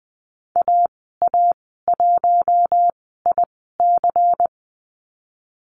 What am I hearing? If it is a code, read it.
Morse code, heard as a keyed tone, about 20 wpm, spaced as for 15 wpm, 700 Hz. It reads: AA1IC
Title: 147 Lesson 24g - Letter y - Call Signs 20wpm